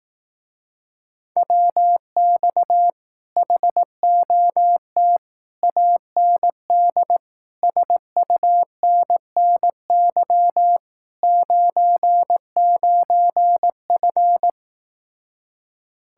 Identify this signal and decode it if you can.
Morse code, heard as a keyed tone, about 18 wpm, 700 Hz. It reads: WX HOT AND SUNNY 99F
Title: Speed-Racing Common QSO Phrases Encore - 15wpm